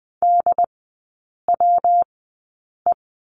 Morse code, heard as a keyed tone, about 20 wpm, spaced as for 8 wpm, 700 Hz. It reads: DWE
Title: 138 Lesson 23e - Letter f - Three Characters 20wpm